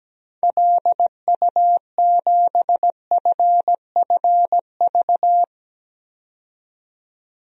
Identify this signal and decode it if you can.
Morse code, heard as a keyed tone, about 17 wpm, 700 Hz. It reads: LU7FFV
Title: DXCC-100 Call Signs - 6 Characters - 17wpm